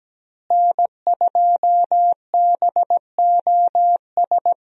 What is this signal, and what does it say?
Morse code, heard as a keyed tone, about 17 wpm, 700 Hz. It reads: N2BOS
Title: USA Call Signs - 5 Characters - Repeated 3x - 17wpm